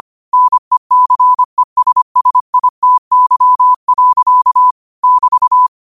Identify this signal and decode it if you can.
Morse code, heard as a keyed tone, about 25 wpm, 1000 Hz. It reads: NECESSITY. <BT>
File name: qotdfiles / 25wpm